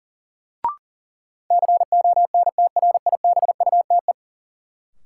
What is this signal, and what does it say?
Morse code, heard as a keyed tone, about 40 wpm, 700 Hz. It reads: CONTRIBUTE
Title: Sentences from Top 1400 Words - New Word Drill - 40wpm